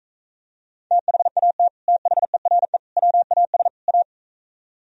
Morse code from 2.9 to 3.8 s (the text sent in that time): WAS